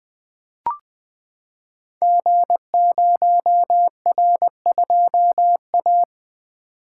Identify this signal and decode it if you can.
Morse code, heard as a keyed tone, about 20 wpm, 700 Hz. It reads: G0R2A
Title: Code Groups - 5 Letter:Numbers - 20wpm